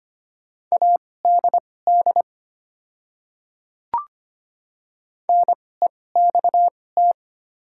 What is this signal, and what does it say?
Morse code, heard as a keyed tone, about 25 wpm, spaced as for 18 wpm, 700 Hz. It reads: ADD NEXT
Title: 2x Character Spacing - Top 500 Words - 25wpm